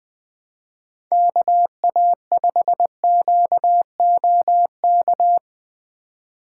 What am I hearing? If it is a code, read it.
Morse code, heard as a keyed tone, about 20 wpm, 700 Hz. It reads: KA5QOK
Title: Speed Racing - USA 2x3 Call Signs 17wpm